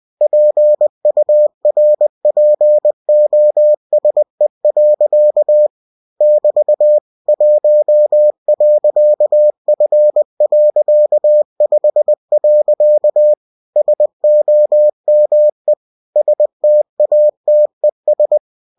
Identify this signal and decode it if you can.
Morse code, heard as a keyed tone, about 20 wpm, 600 Hz. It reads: PURPOSE. <BT> 1.F.5. SOME STATES
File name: cw-books / winnie-the-pooh / winnie-20wpm-page-0119